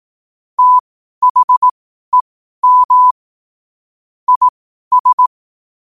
Morse code, heard as a keyed tone, about 18 wpm, spaced as for 12 wpm, 1000 Hz. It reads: THEM IS
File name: qotdfiles / 12wpm